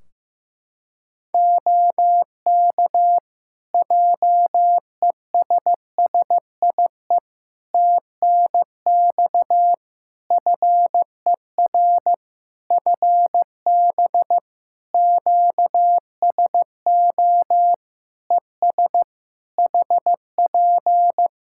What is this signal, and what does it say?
Morse code, heard as a keyed tone, about 15 wpm, 700 Hz. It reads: OK JESSIE TNX FER FB QSO ES HP
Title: Practice QSO - Exchange 4 - Ending - 15wpm